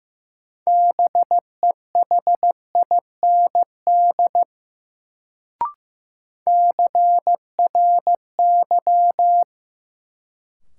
Morse code, heard as a keyed tone, about 15 wpm, 700 Hz. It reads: BEHIND CRY
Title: Sentences from Top 500 Words - New Word Drill - 15wpm